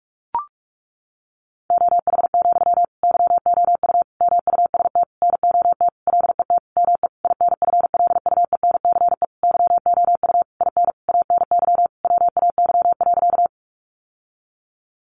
Morse code, heard as a keyed tone, about 44 wpm, 700 Hz. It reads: OH, YOU MUST NOT LET ME INFLUENCE YOU IN ANY WAY.
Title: Sentences from Top 1100 Words 45wpm